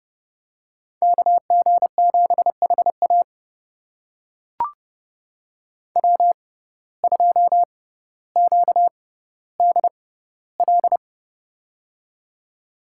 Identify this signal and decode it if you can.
Morse code, heard as a keyed tone, about 30 wpm, spaced as for 15 wpm, 700 Hz. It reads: KG7HA W2QDL
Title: USA Call Signs - 5 Characters - 6x Character Spacing - 30wpm